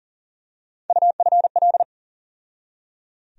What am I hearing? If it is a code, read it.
Morse code, heard as a keyed tone, about 37 wpm, 700 Hz. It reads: UFL
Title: USA Call Sign Postfixes - 3 Letters 40wpm